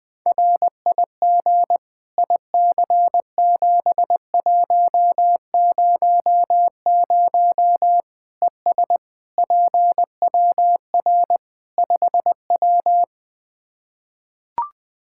Morse code, heard as a keyed tone, about 20 wpm, 700 Hz. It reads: RIG IC7100 ES PWR 5W
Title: Practice QSO - Exchange 2 - Info - 20wpm